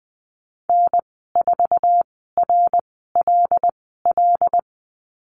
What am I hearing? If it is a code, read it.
Morse code, heard as a keyed tone, about 20 wpm, spaced as for 15 wpm, 700 Hz. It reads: N4RLL
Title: 088 Lesson 15g - Number 5 - Call Signs 20wpm